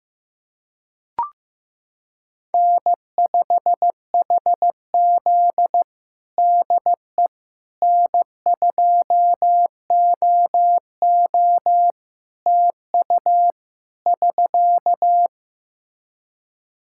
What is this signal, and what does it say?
Morse code, heard as a keyed tone, about 15 wpm, 700 Hz. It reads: N5HZ DE N2OO TU <SK>
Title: Practice QSO - Exchange 4 - Ending - 15wpm